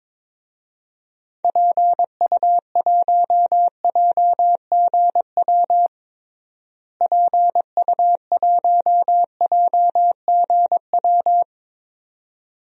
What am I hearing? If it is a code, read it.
Morse code, heard as a keyed tone, about 22 wpm, 700 Hz. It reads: PU1JGW PU1JGW
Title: DXCC-100 Call Signs - 6 Characters - Repeated 2x - 22wpm